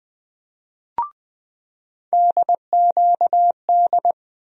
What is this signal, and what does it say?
Morse code, heard as a keyed tone, about 20 wpm, 700 Hz. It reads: DQD
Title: USA Call Sign Postfixes - 3 Letters 20wpm